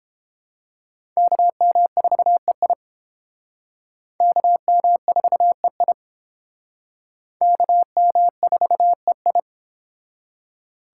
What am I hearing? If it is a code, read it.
Morse code, heard as a keyed tone, about 29 wpm, 700 Hz. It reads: KM4EI KM4EI KM4EI